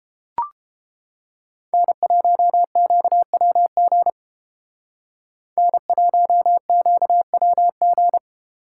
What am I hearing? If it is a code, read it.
Morse code, heard as a keyed tone, about 31 wpm, 700 Hz. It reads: N1QWG N1QWG